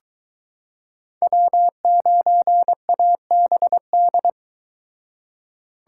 Morse code, heard as a keyed tone, about 24 wpm, 700 Hz. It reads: W9ABD